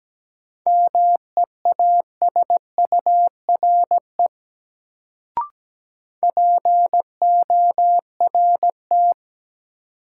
Morse code, heard as a keyed tone, about 17 wpm, 700 Hz. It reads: MEASURE PORT
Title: Sentences from Top 1000 Words - Review Part 1 - New Word Drill - 17wpm